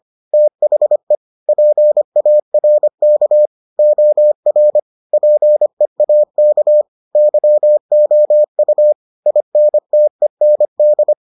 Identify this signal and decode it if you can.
Morse code, heard as a keyed tone, about 25 wpm, 600 Hz. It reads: THE PARK OR PEAK YOU INTEND